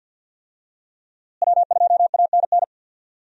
Morse code, heard as a keyed tone, about 53 wpm, 700 Hz. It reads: W2ANN
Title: USA Call Signs - 5 Characters - Repeated 3x - 50wpm